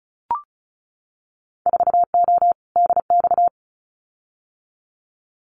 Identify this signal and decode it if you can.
Morse code, heard as a keyed tone, about 34 wpm, 700 Hz. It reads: 4O DX
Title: Winter Field Day Exchanges Encore 35wpm